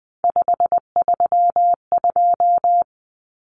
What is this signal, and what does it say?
Morse code, heard as a keyed tone, about 20 wpm, 700 Hz. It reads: 532
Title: Common Cut Numbers - 3-Digit 20wpm